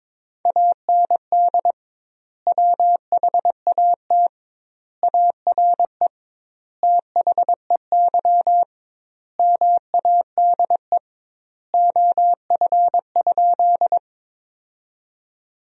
Morse code, heard as a keyed tone, about 22 wpm, 700 Hz. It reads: AND WHAT ARE THEY MADE OF?
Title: 2x Word-Spacing - Sentences from Top 200 Words - 22wpm